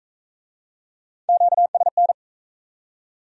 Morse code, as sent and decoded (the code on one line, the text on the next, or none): --.- ... -.
QSN